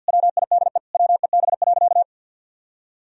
Morse code, heard as a keyed tone, about 50 wpm, 700 Hz. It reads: WIDE WEB.
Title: Sentences from Top 1000 Words - Review Part 4 - 50wpm